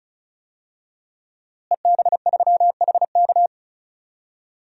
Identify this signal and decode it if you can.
Morse code, heard as a keyed tone, about 36 wpm, 700 Hz. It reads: EB3HK
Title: DXCC-100 Call Signs - 5 Characters - Repeated 2x - 35wpm